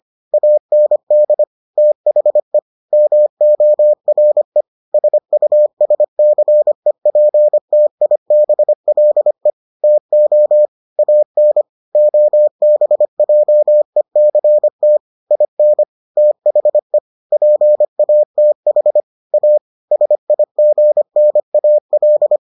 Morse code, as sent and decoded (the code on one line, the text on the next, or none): .- -. -.. / - .... / -- --- .-. / ... ..- ... -.-. .--. - .. -... .-.. / - --- / .- -. / --- -... .--- . -.-. - / .. -. / - .... / .--. .- - .... / .- / ... .. --. -. .- .-..
AND THE MORE SUSCEPTIBLE TO AN OBJECT IN THE PATH A SIGNAL